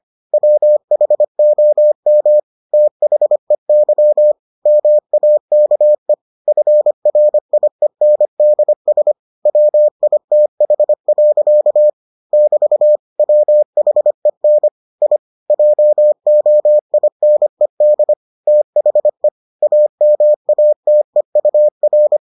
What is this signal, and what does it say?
Morse code, heard as a keyed tone, about 25 wpm, 600 Hz. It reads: WHOM THEY MAKE FRIENDS WITH. <BT> WHEN I JOINED THE AMATEUR